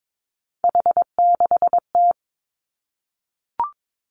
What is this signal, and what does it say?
Morse code, heard as a keyed tone, about 22 wpm, 700 Hz. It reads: H6T
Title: Three Challenging Letter-Number Combinations 22wpm